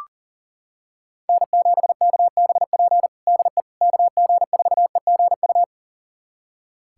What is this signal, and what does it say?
Morse code, heard as a keyed tone, about 40 wpm, 700 Hz. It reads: N7KBP DE KG4EGU
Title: USA Callsign DE Callsign 40wpm